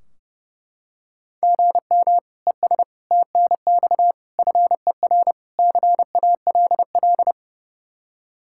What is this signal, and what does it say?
Morse code, heard as a keyed tone, about 30 wpm, 700 Hz. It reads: GM ES TNX FER CALL
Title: Practice QSO - Exchange 1 - Beginning - 30wpm